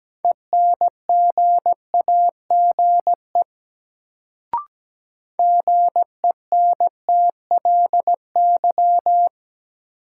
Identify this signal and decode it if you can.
Morse code, heard as a keyed tone, about 17 wpm, 700 Hz. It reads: ENGAGE GENTLY
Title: Sentences from Top 1500 Words - New Word Drill Alt - 17wpm